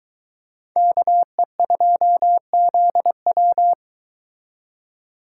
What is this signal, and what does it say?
Morse code, heard as a keyed tone, about 21 wpm, 700 Hz. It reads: KE2ZW
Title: Speed Racing - USA 2x2 Call Signs 17wpm